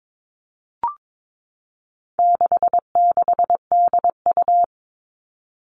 Morse code, heard as a keyed tone, about 22 wpm, 700 Hz. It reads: 66DU
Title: All Cut Numbers - 4-Digit 22wpm